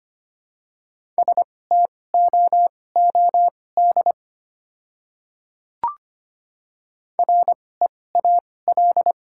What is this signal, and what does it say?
Morse code, heard as a keyed tone, about 25 wpm, spaced as for 18 wpm, 700 Hz. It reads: STOOD REAL
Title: 2x Character Spacing - Top 500 Words - 25wpm